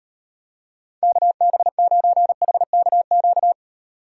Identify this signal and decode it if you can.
Morse code, heard as a keyed tone, about 38 wpm, 700 Hz. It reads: KB9HKQ